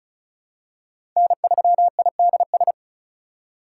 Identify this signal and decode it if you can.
Morse code, heard as a keyed tone, about 36 wpm, 700 Hz. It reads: N3IDS
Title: USA Call Signs - 5 Characters - 3x Character Spacing - 35wpm